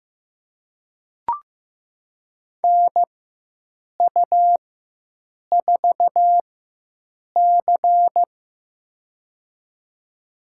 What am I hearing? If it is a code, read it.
Morse code, heard as a keyed tone, about 15 wpm, spaced as for 7 wpm, 700 Hz. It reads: NU4C
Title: USA Call Signs - 4 Characters - 4x Character Spacing - 15wpm